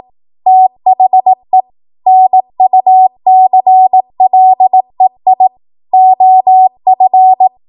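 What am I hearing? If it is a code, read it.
Morse code, heard as a keyed tone, about 18 wpm, 750 Hz. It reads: THE NUCLEI OF
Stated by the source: Code practice at 18 WPM, a transition run bridging 15 to 20 WPM.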